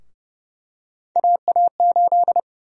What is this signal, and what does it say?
Morse code, heard as a keyed tone, about 30 wpm, 700 Hz. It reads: AA8
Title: USA Call Sign Prefixes - 30wpm